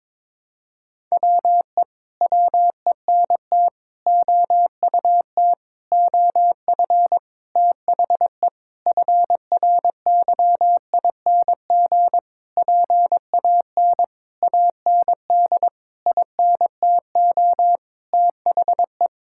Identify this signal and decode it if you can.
Morse code, heard as a keyed tone, about 22 wpm, 700 Hz. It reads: WE WENT OUT OF THE FRYING PAN AND INTO THE
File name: Sentences with 9-Word Phrases - 22wpm